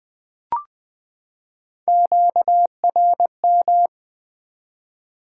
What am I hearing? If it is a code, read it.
Morse code, heard as a keyed tone, about 20 wpm, 700 Hz. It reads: QRM
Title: Top 90 QSO Elements 20wpm